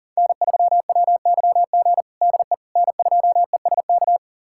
Morse code, heard as a keyed tone, about 40 wpm, 700 Hz. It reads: N3WYG DE N2ESK
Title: USA Callsign DE Callsign - 5 Characters - 40wpm